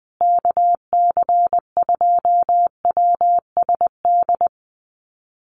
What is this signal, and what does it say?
Morse code, heard as a keyed tone, about 20 wpm, 700 Hz. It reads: KC2WSD
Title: USA Call Signs Repeated 3x 20wpm